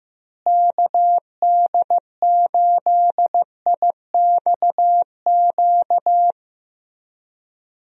Morse code, heard as a keyed tone, about 15 wpm, 700 Hz. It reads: KD8IXQ
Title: USA Call Signs - 6 Characters - Repeated 2x - 15wpm